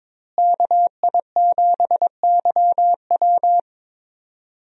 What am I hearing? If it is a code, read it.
Morse code, heard as a keyed tone, about 21 wpm, 700 Hz. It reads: KI7YW
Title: USA Call Signs - 5 Characters - Repeated 3x - 22wpm